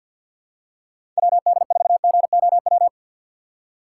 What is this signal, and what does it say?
Morse code, heard as a keyed tone, about 53 wpm, 700 Hz. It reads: WD4GOW
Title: USA Call Signs - 6 Characters - 4x Character Spacing - 50wpm